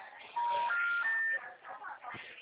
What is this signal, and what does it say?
Special information tone, with speech.